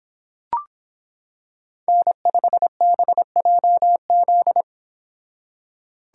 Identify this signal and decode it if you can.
Morse code, heard as a keyed tone, about 25 wpm, 700 Hz. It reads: N5BJZ